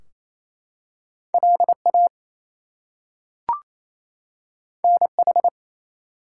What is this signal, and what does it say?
Morse code, heard as a keyed tone, about 28 wpm, 700 Hz. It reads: LA NH
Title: US States and Canadian Provinces 28wpm